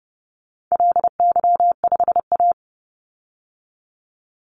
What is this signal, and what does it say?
Morse code, heard as a keyed tone, about 30 wpm, 700 Hz. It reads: LY5A
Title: Call Signs Repeated 2x - 30wpm